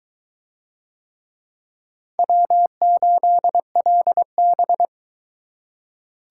Morse code, heard as a keyed tone, about 21 wpm, 700 Hz. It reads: W8LB